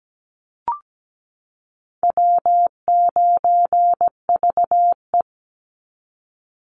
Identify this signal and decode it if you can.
Morse code, heard as a keyed tone, about 17 wpm, 700 Hz. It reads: W9VE
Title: Call Signs - 17wpm